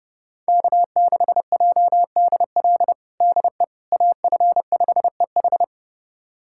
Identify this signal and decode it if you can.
Morse code, heard as a keyed tone, about 32 wpm, 700 Hz. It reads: K6JDL DE AF5EH
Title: USA Callsign DE Callsign - 5 Characters - 30wpm